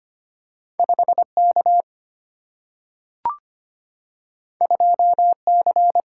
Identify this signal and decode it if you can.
Morse code, heard as a keyed tone, about 25 wpm, 700 Hz. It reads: 5 K 2 C
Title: Single-Letter Single-Number 25wpm